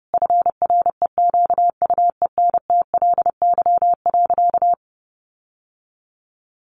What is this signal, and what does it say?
Morse code, heard as a keyed tone, about 28 wpm, 700 Hz. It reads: FREQUENTLY.
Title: Sentences from Top 1300 Words 30wpm